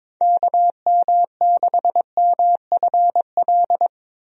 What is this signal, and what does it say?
Morse code, heard as a keyed tone, about 22 wpm, 700 Hz. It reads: KM6MFL
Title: USA Call Signs - 6 Characters - Repeated 2x - 22wpm